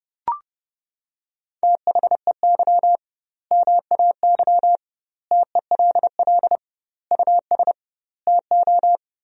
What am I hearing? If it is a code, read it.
Morse code, heard as a keyed tone, about 30 wpm, 700 Hz. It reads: THEY MAY TELL US TO
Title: 2x Word-Spacing - Sentences from Top 300 Words - 30wpm